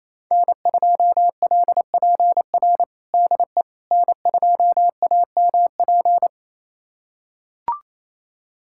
Morse code, heard as a keyed tone, about 28 wpm, 700 Hz. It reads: N2LPR DE N2AMP
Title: USA Callsign DE Callsign - 5 Characters - 28wpm